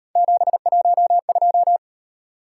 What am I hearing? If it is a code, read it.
Morse code, heard as a keyed tone, about 37 wpm, 700 Hz. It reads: 712